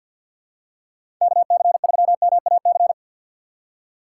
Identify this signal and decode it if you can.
Morse code, heard as a keyed tone, about 53 wpm, 700 Hz. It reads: KC3MAC
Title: USA Call Signs - 6 Characters - 4x Character Spacing - 50wpm